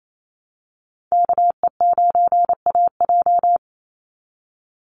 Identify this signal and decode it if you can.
Morse code, heard as a keyed tone, about 28 wpm, 700 Hz. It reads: KE9AJ
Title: NA SOTA Chaser Callsigns 28wpm